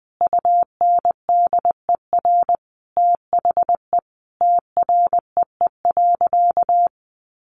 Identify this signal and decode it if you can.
Morse code, heard as a keyed tone, about 20 wpm, 700 Hz. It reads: UNDER THE TREE.
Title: Sentences from Top 1400 Words 20wpm